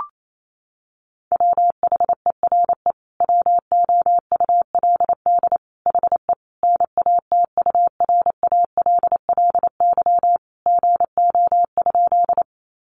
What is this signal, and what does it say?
Morse code, heard as a keyed tone, about 28 wpm, 700 Hz. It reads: WHERE WOULD HE NATURALLY GO?
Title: Sentences from Top 1300 Words 28wpm